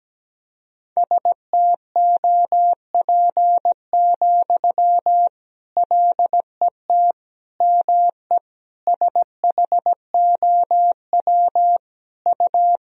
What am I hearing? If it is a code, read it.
Morse code, heard as a keyed tone, about 17 wpm, 700 Hz. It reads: STOP, LET ME SHOW U
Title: Sentences from Top 300 Words and Abbreviations 17wpm